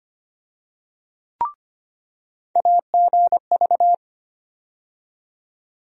Morse code, heard as a keyed tone, about 25 wpm, 700 Hz. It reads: AGV
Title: USA Call Sign Postfixes - 3 Letters 25wpm